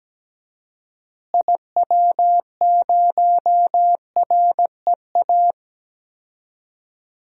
Morse code, heard as a keyed tone, about 17 wpm, 700 Hz. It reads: IW0REA
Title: DXCC-100 Call Signs - 6 Characters - Repeated 2x - 17wpm